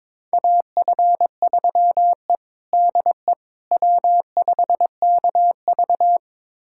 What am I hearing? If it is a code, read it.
Morse code, heard as a keyed tone, about 22 wpm, 700 Hz. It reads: AF3E DE W5KV
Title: USA Callsign DE Callsign - 4 Characters - 22wpm